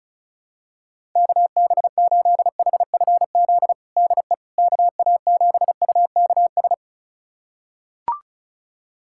Morse code, heard as a keyed tone, about 34 wpm, 700 Hz. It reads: KB8HFZ DE KA7UKS
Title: USA Callsign DE Callsign - 6 Characters - 35wpm